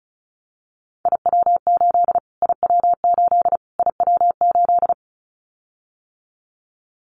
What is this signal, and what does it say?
Morse code, heard as a keyed tone, about 34 wpm, 700 Hz. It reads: IW8 IW8 IW8
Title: Callsign Prefixes for Top 25 DXCC - Repeated 3x - 35wpm